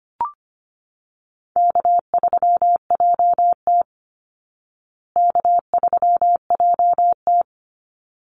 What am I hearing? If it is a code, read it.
Morse code, heard as a keyed tone, about 25 wpm, 700 Hz. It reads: K3JT K3JT
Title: Easy Call Signs Repeated 2x - 25wpm